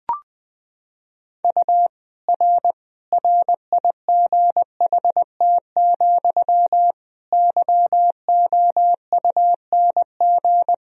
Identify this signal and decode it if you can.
Morse code, heard as a keyed tone, about 20 wpm, 700 Hz. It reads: U R RIGHT, YOUNG
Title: Sentences from Top 400 Words and Abbreviations 20wpm